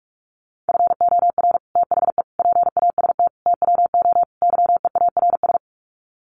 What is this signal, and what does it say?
Morse code, heard as a keyed tone, about 44 wpm, 700 Hz. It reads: FOR THE PAST TWO YEARS
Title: Sets of 5 Words Encore2 45wpm